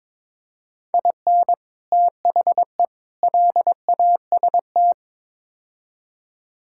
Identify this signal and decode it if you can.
Morse code, heard as a keyed tone, about 22 wpm, 700 Hz. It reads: IN THE LAST
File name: Build A Phrase - 5-Word Phrases - 22wpm